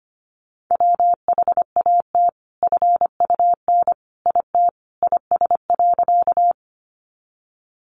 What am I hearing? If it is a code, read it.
Morse code, heard as a keyed tone, about 25 wpm, 700 Hz. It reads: WHAT FUN IT IS.
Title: Sentences from Top 700 Words 25wpm